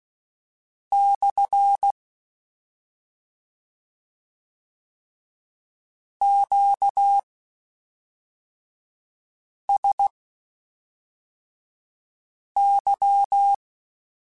Morse code, heard as a keyed tone, about 16 wpm, spaced as for 3 wpm, 800 Hz. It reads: / QSY